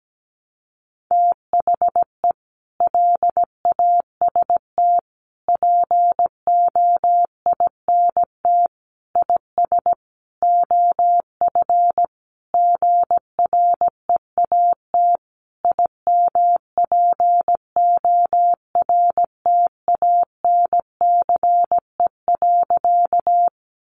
Morse code, heard as a keyed tone, about 17 wpm, 700 Hz. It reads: THE LAST POINT IS OF GREAT IMPORTANCE.
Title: Sentences from Top 1200 Words 17wpm